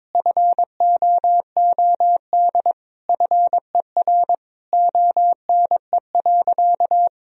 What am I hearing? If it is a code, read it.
Morse code, heard as a keyed tone, about 22 wpm, 700 Hz. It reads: FOOD FER ONE.
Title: Sentences from Top 300 Words and Abbreviations 22wpm